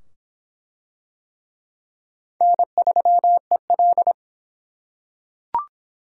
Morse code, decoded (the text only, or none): N3EL